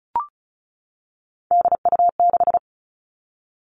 Morse code, heard as a keyed tone, about 36 wpm, 700 Hz. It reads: DU6
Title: Three Challenging Letter-Number Combinations 35wpm